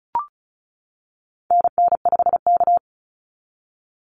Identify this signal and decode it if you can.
Morse code, heard as a keyed tone, about 34 wpm, 700 Hz. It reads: NN5K
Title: NA SOTA Activator Callsigns 35wpm